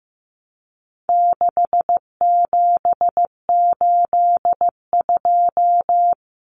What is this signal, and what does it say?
Morse code, heard as a keyed tone, about 15 wpm, 700 Hz. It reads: 6782
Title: Common Cut Numbers - 4-Digit 15wpm